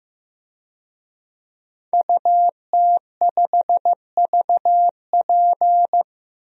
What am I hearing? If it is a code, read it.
Morse code, heard as a keyed tone, about 15 wpm, 700 Hz. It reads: UT5VP
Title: DXCC-100 Call Signs - 5 Characters - Repeated 3x - 15wpm